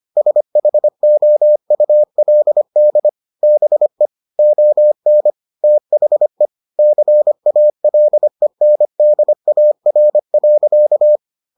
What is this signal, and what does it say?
Morse code, heard as a keyed tone, about 25 wpm, 600 Hz. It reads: SHOULD BE ON THE CALENDAR.